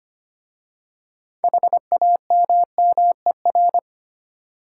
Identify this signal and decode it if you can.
Morse code, heard as a keyed tone, about 25 wpm, 700 Hz. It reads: HAMMER
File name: practice / 25wpm / Easy Words with Hints - 25wpm